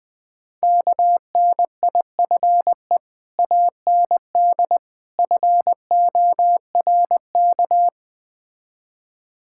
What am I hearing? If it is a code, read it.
Morse code, heard as a keyed tone, about 20 wpm, 700 Hz. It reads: KNIFE AND FORK
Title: Easy Words - Speed-Racing with Hints - 20wpm